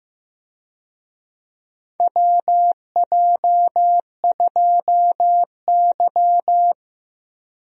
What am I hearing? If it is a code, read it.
Morse code, heard as a keyed tone, about 15 wpm, 700 Hz. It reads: WJ2Y
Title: USA Call Signs - 4 Characters - Repeated 2x - 15wpm